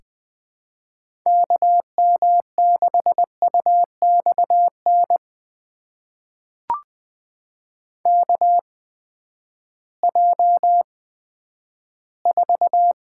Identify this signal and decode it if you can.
Morse code, heard as a keyed tone, about 20 wpm, spaced as for 11 wpm, 700 Hz. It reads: KM6UXN KJ4
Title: USA Call Signs - 6 Characters - 8x Character Spacing - 20wpm